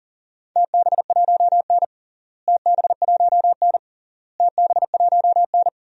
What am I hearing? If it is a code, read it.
Morse code, heard as a keyed tone, about 40 wpm, 700 Hz. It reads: TB1N TB1N TB1N
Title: DXCC-100 Call Signs - 4 Characters - Repeated 3x - 40wpm